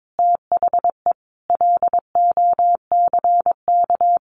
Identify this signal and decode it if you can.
Morse code, heard as a keyed tone, about 22 wpm, 700 Hz. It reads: THE LOCK
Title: Sentences from Top 1300 Words 22wpm